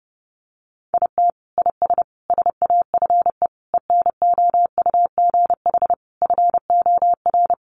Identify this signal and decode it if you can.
Morse code, heard as a keyed tone, about 30 wpm, 700 Hz. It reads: IT IS SAFE ENOUGH FOR
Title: Sentences from Top 900 Words 30wpm